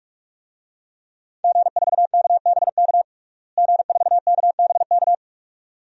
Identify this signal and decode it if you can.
Morse code, heard as a keyed tone, about 46 wpm, 700 Hz. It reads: G4KBK G4KBK
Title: DXCC-100 Call Signs - 5 Characters - Repeated 2x - 45wpm